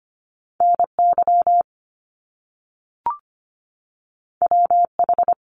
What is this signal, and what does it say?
Morse code, heard as a keyed tone, about 25 wpm, 700 Hz. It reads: NY WH
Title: Top Two Letter Combinations in Words 25wpm